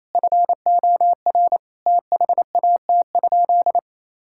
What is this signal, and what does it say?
Morse code, heard as a keyed tone, about 28 wpm, 700 Hz. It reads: FOR THAT?
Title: Sentences from Top 1000 Words - Review Part 6 - 28wpm